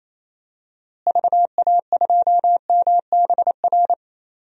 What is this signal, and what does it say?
Morse code, heard as a keyed tone, about 27 wpm, 700 Hz. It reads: VA2MBR
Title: DXCC-100 Call Signs - 6 Characters - Repeated 3x - 28wpm